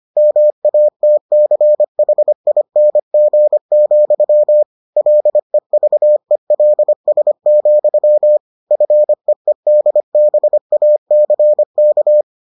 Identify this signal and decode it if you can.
Morse code, heard as a keyed tone, about 25 wpm, 600 Hz. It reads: MATCHING, LEVELS, FEEDBACK